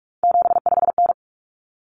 Morse code, heard as a keyed tone, about 44 wpm, 700 Hz. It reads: 75N
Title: Common Cut Numbers - 3-Digit 45wpm